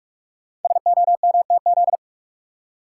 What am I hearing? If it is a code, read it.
Morse code, heard as a keyed tone, about 44 wpm, 700 Hz. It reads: SOMTZ
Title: Code Groups - 5 Letter:Numbers - 45wpm